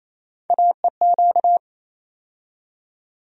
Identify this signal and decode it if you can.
Morse code, heard as a keyed tone, about 28 wpm, 700 Hz. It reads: AEQ
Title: USA Call Sign Postfixes - 3 Letters 28wpm